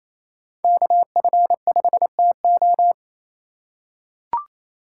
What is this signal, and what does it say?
Morse code, heard as a keyed tone, about 28 wpm, 700 Hz. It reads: KF5TO
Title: USA Call Signs - 5 Characters - 4x Character Spacing - 28wpm